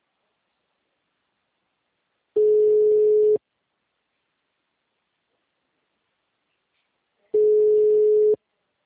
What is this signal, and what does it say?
Ringback tone (Europe, 425 Hz).
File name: out- -2030- - - .39196